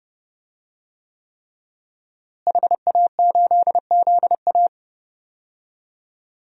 Morse code, heard as a keyed tone, about 32 wpm, 700 Hz. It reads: HA8ZA
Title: DXCC-100 Call Signs - 5 Characters - Repeated 3x - 30wpm